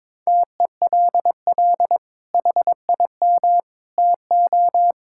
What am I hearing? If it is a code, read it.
Morse code, heard as a keyed tone, about 22 wpm, 700 Hz. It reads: TELL HIM TO
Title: Sentences from Top 1000 Words - Review Part 6 - 22wpm